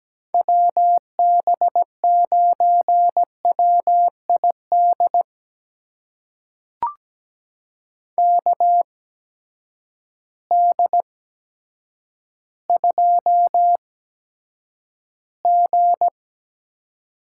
Morse code, heard as a keyed tone, about 17 wpm, spaced as for 8 wpm, 700 Hz. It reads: WB9WID KD2G
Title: USA Call Signs - 6 Characters - 8x Character Spacing - 17wpm